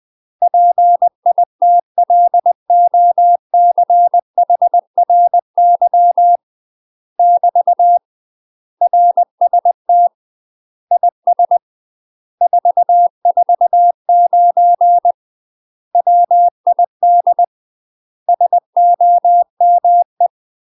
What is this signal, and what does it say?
Morse code, heard as a keyed tone, about 20 wpm, 700 Hz. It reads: PITLOCHRY <BT> RST IS 449 WID SOME
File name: UK MASTER QSOs TRAINING / 20WPM / UK MASTER QSOs 0071